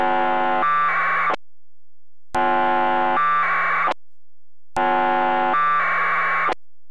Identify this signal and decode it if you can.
Packet radio (AX.25, as in APRS).